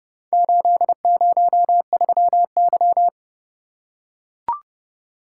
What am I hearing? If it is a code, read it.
Morse code, heard as a keyed tone, about 31 wpm, 700 Hz. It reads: 803Y